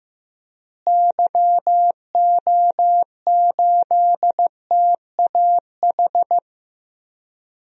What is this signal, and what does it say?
Morse code, heard as a keyed tone, about 16 wpm, 700 Hz. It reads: YO8TAH
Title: DXCC-100 Call Signs - 6 Characters - Repeated 2x - 15wpm